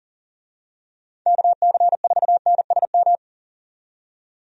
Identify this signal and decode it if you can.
Morse code, heard as a keyed tone, about 38 wpm, 700 Hz. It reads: KC4NSM
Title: USA Call Signs - 6 Characters - 40wpm